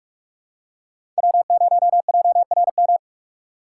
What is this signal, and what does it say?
Morse code, heard as a keyed tone, about 41 wpm, 700 Hz. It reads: W0JRM
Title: USA Call Signs - 5 Characters - 4x Character Spacing - 45wpm